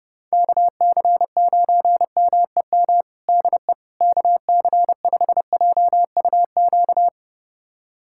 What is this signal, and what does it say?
Morse code, heard as a keyed tone, about 30 wpm, 700 Hz. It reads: KC9MEM DE KC5JUQ
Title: USA Callsign DE Callsign - 6 Characters - 30wpm